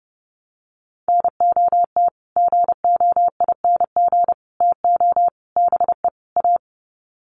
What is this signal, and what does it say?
Morse code, heard as a keyed tone, about 30 wpm, 700 Hz. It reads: NOT GOING TO BE A
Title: Sets of 5 Words Encore 30wpm